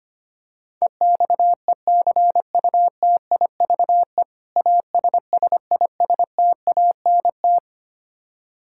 Morse code, heard as a keyed tone, about 25 wpm, 700 Hz. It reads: EXECUTIVE ASSISTANT
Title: Practice QSO - Exchange 3 - More Info - 25wpm